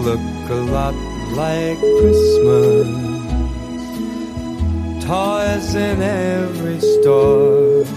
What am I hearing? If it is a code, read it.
Ringback tone (Europe, 425 Hz), with noise.